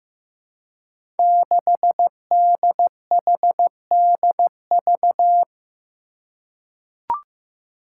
Morse code, heard as a keyed tone, about 15 wpm, 700 Hz. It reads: 6DHDV
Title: Five Challenging Letter-Number Combinations 15wpm